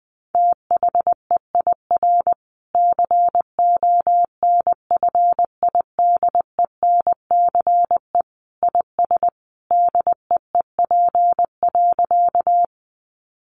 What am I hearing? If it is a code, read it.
Morse code, heard as a keyed tone, about 20 wpm, 700 Hz. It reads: THEIR CONFIDENCE IS DEEP.
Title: Sentences from Top 1500 Words 20wpm